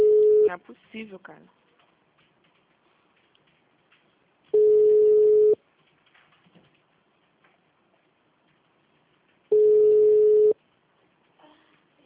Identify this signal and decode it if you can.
Ringback tone (Europe, 425 Hz), with speech.
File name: out- -2001- - - .9515